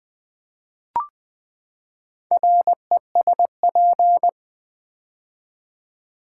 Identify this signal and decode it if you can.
Morse code, heard as a keyed tone, about 20 wpm, 700 Hz. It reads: RESP
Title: Build A Word - 4 Letters At A Time - 12-Letter Words - 20wpm